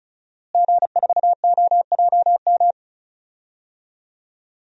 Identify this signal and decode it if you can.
Morse code, heard as a keyed tone, about 36 wpm, 700 Hz. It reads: G4OJM
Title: DXCC-100 Call Signs - 5 Characters - Repeated 3x - 35wpm